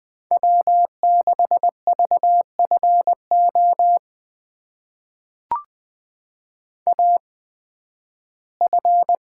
Morse code, heard as a keyed tone, about 20 wpm, spaced as for 13 wpm, 700 Hz. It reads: W6VFO AF
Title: USA Call Signs - 5 Characters - 8x Character Spacing - 20wpm